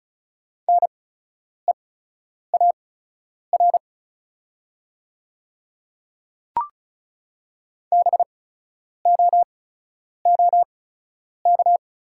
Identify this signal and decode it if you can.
Morse code, heard as a keyed tone, about 36 wpm, spaced as for 10 wpm, 700 Hz. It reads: NEAR BOOK
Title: 8x Character Spacing - Top 300 Words - 35wpm